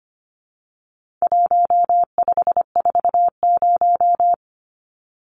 Morse code, heard as a keyed tone, about 25 wpm, 700 Hz. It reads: 1540
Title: Numbers - 4-Digit 25wpm